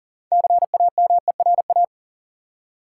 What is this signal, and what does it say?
Morse code, heard as a keyed tone, about 40 wpm, 700 Hz. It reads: CAMERA
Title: Six Letter Words - 40wpm